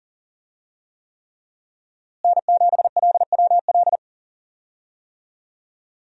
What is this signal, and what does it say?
Morse code, heard as a keyed tone, about 40 wpm, 700 Hz. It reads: N7LWL
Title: USA Call Signs - 5 Characters - Repeated 3x - 40wpm